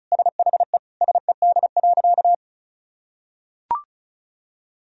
Morse code, heard as a keyed tone, about 34 wpm, 700 Hz. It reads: SHE SED.
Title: Sentences from Top 500 Words and Abbreviations 35wpm